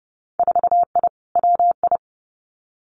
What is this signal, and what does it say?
Morse code, heard as a keyed tone, about 30 wpm, 700 Hz. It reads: 4I WI
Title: Winter Field Day Exchanges 30wpm